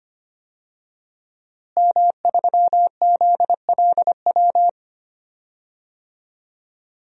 Morse code, heard as a keyed tone, about 25 wpm, 700 Hz. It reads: M3ZLW
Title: DXCC-100 Call Signs - 5 Characters - Repeated 2x - 25wpm